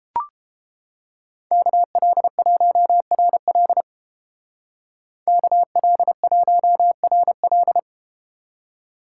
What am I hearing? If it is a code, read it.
Morse code, heard as a keyed tone, about 32 wpm, 700 Hz. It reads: KL1RL KL1RL